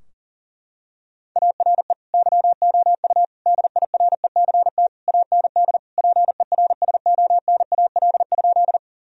Morse code, heard as a keyed tone, about 37 wpm, 700 Hz. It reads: ARE YOU DIRECT AND PERSONAL?
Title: Sentences from Top 1000 Words - Review Part 4 - 40wpm